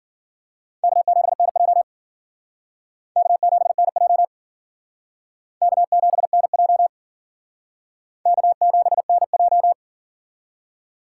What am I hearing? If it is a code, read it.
Morse code, heard as a keyed tone, about 52 wpm, 700 Hz. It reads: K7NJ K7NJ K7NJ K7NJ